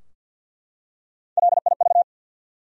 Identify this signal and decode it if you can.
Morse code, heard as a keyed tone, about 52 wpm, 700 Hz. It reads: LIV
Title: USA Common Names - Max 3 Letters - 50wpm